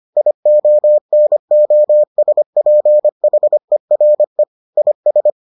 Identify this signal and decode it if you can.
Morse code, heard as a keyed tone, about 25 wpm, 600 Hz. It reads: IONOSPHERE IS